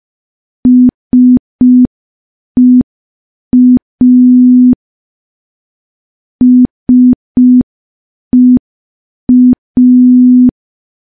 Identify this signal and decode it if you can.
Morse code, heard as a keyed tone, about 5 wpm, 250 Hz. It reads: SEA SEA
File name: morse / morse-training-05WPM